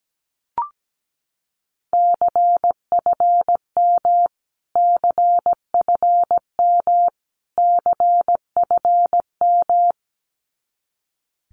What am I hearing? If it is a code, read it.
Morse code, heard as a keyed tone, about 17 wpm, 700 Hz. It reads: CFM CFM CFM
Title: Abbreviations Repeated 3x - 17wpm